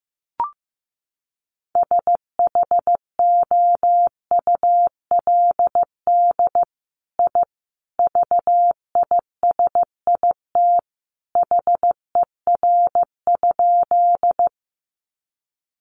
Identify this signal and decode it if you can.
Morse code, heard as a keyed tone, about 15 wpm, 700 Hz. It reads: SHOULD I VISIT HER?
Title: Sentences from Top 900 Words 15wpm